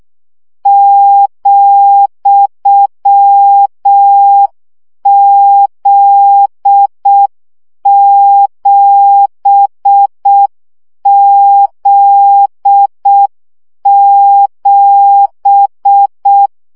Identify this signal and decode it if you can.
Morse code, heard as a keyed tone, about 6 wpm, 800 Hz. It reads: ,Z7Z7
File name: CW gyakorlas z 7 vesszo 5 baud